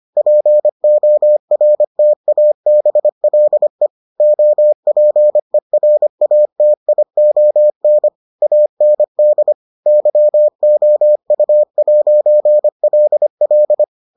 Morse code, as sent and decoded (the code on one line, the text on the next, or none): .--. --- .-. - .- -... .-.. / --- .--. .-. .- - .. --- -. / .- -. -.. / -.-- --- ..- .----. .-.. .-..
PORTABLE OPERATION AND YOU'LL